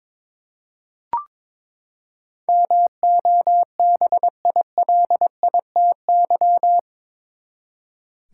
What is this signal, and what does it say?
Morse code, heard as a keyed tone, about 22 wpm, 700 Hz. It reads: MOBILITY
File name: practice / 22wpm / Eight Letter Words - 22wpm